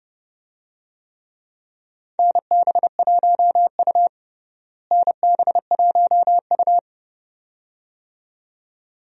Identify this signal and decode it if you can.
Morse code, heard as a keyed tone, about 30 wpm, 700 Hz. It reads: NB1U NB1U
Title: USA Call Signs - 4 Characters - Repeated 2x - 30wpm